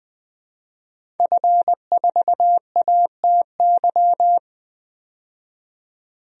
Morse code, heard as a keyed tone, about 20 wpm, 700 Hz. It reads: F4ATY
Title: DXCC-100 Call Signs - 5 Characters - 20wpm